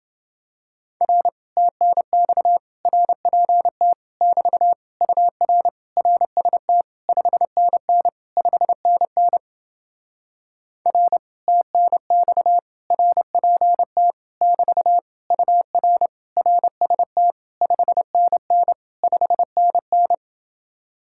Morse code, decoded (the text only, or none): R TNX RPT <BT> UR RST 5NN 5NN R TNX RPT <BT> UR RST 5NN 5NN